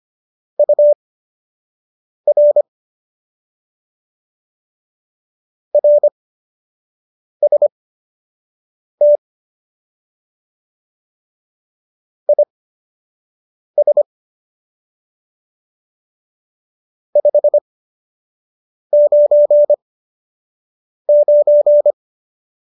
Morse code, heard as a keyed tone, about 25 wpm, spaced as for 6 wpm, 590 Hz. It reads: UR RST IS 599